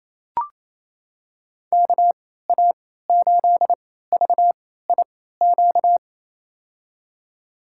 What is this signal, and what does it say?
Morse code, heard as a keyed tone, about 28 wpm, spaced as for 16 wpm, 700 Hz. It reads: KA8VIQ
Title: USA Call Signs - 6 Characters - 3x Character Spacing - 28wpm